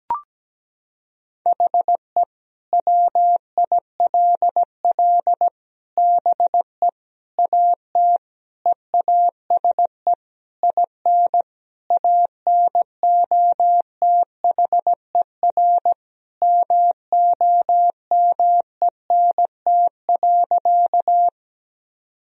Morse code, heard as a keyed tone, about 17 wpm, 700 Hz. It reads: HE WILL BE AT EASE IN ANOTHER MOMENT.
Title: Sentences from Top 1000 Words - Review Part 1 - 17wpm